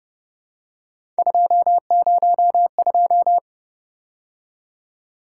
Morse code, heard as a keyed tone, about 30 wpm, 700 Hz. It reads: 202